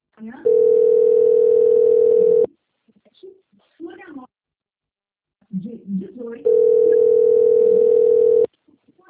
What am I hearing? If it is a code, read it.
Ringback tone (North America), 440 + 480 Hz, with speech.
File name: out- - - - - .24240